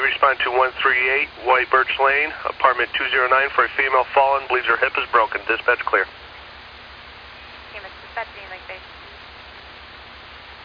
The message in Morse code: none

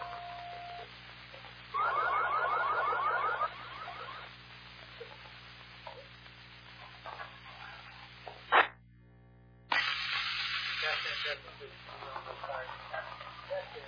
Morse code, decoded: none

